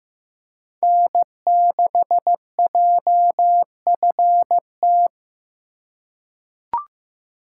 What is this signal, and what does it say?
Morse code, heard as a keyed tone, about 15 wpm, 700 Hz. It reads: N6JFT
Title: USA Call Signs - 5 Characters - 3x Character Spacing - 15wpm